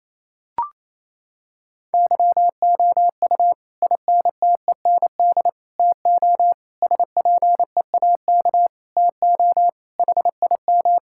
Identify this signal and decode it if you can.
Morse code, heard as a keyed tone, about 28 wpm, 700 Hz. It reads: YOU INTEND TO SPEAK TO HIM
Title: Sentences from Top 1600 Words 28wpm